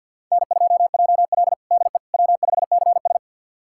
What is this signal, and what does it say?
Morse code, heard as a keyed tone, about 49 wpm, 700 Hz. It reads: N2JL DE W5CS
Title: USA Callsign DE Callsign - 4 Characters - 50wpm